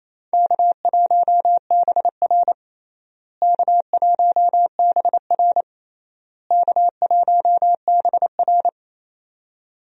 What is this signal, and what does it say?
Morse code, heard as a keyed tone, about 28 wpm, 700 Hz. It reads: K1BR K1BR K1BR
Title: USA Call Signs - 4 Characters - Repeated 3x - 28wpm